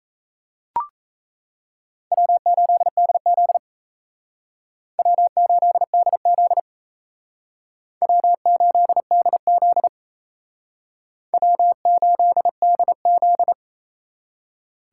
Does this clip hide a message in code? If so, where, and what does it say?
Yes: Morse code from 2.1 to 13.5 s, heard as a keyed tone, about 37 wpm, 700 Hz. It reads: W8DZ W8DZ W8DZ W8DZ